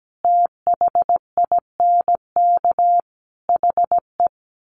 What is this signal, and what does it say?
Morse code, heard as a keyed tone, about 17 wpm, 700 Hz. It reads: THINK HE
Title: Sets of 4 Words 17wpm